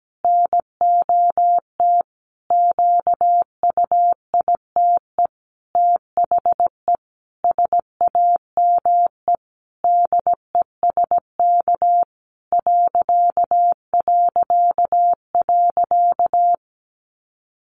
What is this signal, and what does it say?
Morse code, heard as a keyed tone, about 17 wpm, 700 Hz. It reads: NOT QUITE THE SAME DESK ...
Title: A Sound Of Thunder 17wpm